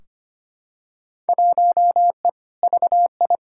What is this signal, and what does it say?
Morse code, heard as a keyed tone, about 25 wpm, 700 Hz. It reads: 1E VI
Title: Field Day Exchanges Encore - 25wpm